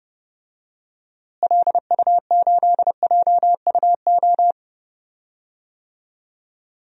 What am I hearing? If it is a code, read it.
Morse code, heard as a keyed tone, about 31 wpm, 700 Hz. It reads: LU8JUO
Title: DXCC-100 Call Signs - 6 Characters - Repeated 3x - 30wpm